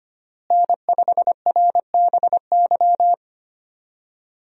Morse code, heard as a keyed tone, about 25 wpm, 700 Hz. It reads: N5RBY
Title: USA Call Signs - 5 Characters - Repeated 3x - 25wpm